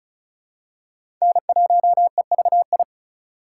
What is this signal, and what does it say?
Morse code, heard as a keyed tone, about 36 wpm, 700 Hz. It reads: N1EVI
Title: USA Call Signs - 5 Characters - 3x Character Spacing - 35wpm